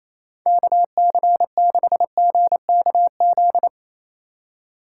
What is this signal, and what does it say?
Morse code, heard as a keyed tone, about 27 wpm, 700 Hz. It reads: KC6GKZ
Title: USA Call Signs - 6 Characters - 4x Character Spacing - 28wpm